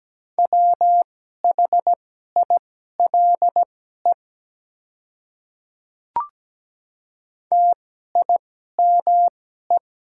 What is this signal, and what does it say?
Morse code, heard as a keyed tone, about 17 wpm, spaced as for 12 wpm, 700 Hz. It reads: WHILE TIME
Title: 2x Character Spacing - Top 300 Words - 17wpm